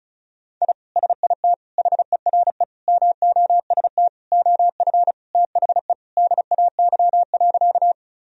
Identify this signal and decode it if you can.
Morse code, heard as a keyed tone, about 36 wpm, 700 Hz. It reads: I SIT HERE MOST OF THE DAY.
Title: Sentences from Top 1000 Words - Review Part 1 - 35wpm